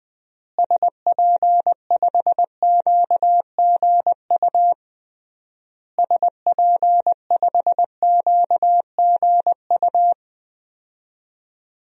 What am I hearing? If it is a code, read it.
Morse code, heard as a keyed tone, about 20 wpm, 700 Hz. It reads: SP5QGU SP5QGU
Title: DXCC-100 Call Signs - 6 Characters - Repeated 3x - 20wpm